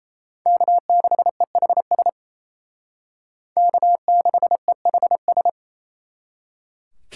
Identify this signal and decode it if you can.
Morse code, heard as a keyed tone, about 30 wpm, 700 Hz. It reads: K6EHS K6EHS
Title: Speed Racing - USA 1x3 Call Signs 28wpm